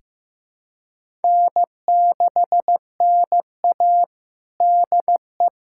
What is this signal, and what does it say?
Morse code, heard as a keyed tone, about 15 wpm, 700 Hz. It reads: N6NA DE
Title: USA Callsign DE Callsign - 4 Characters - 15wpm